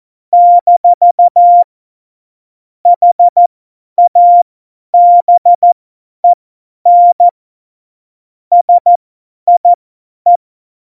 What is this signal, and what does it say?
Morse code, heard as a keyed tone, about 14 wpm, spaced as for 10 wpm, 700 Hz. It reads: - HABEN SIE
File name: Morsekurs / Witze / Witz 15. Januar 2019, 50 F70